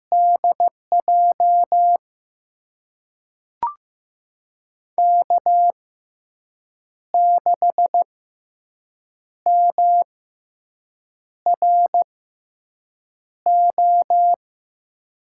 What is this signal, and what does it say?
Morse code, heard as a keyed tone, about 15 wpm, spaced as for 6 wpm, 700 Hz. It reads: DJ K6MRO
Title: USA Call Signs - 6x Character Spacing - 15wpm